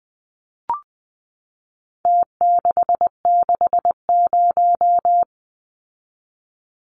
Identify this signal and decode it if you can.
Morse code, heard as a keyed tone, about 20 wpm, 700 Hz. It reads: T660
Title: Numbers - Contest Style - 4-Digit 20wpm